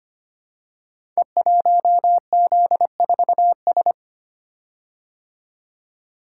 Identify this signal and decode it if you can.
Morse code, heard as a keyed tone, about 25 wpm, 700 Hz. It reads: E1Z4S
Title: Code Groups - 5 Letter:Numbers - 25wpm